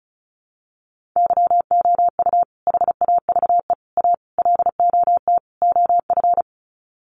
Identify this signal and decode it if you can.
Morse code, heard as a keyed tone, about 34 wpm, 700 Hz. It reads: YOU HAVE A LOT OF
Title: Sets of 5 Words Encore 35wpm